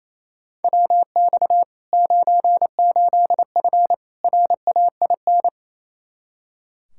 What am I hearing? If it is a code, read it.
Morse code, heard as a keyed tone, about 27 wpm, 700 Hz. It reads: WX 98F RAIN
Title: Practice QSO - Exchange 2 - Info - 28wpm